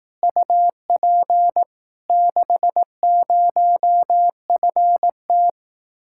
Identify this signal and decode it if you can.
Morse code, heard as a keyed tone, about 18 wpm, 700 Hz. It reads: UP 60FT
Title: Speed-Racing Common QSO Phrases Encore - 15wpm